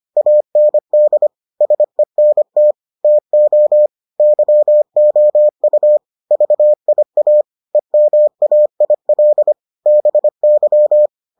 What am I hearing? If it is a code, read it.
Morse code, heard as a keyed tone, about 25 wpm, 600 Hz. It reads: AND SENT TO YOU VIA EMAIL BY